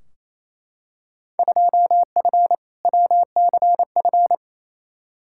Morse code, heard as a keyed tone, about 28 wpm, 700 Hz. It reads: 2F WCF
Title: Field Day Exchanges Encore - 28wpm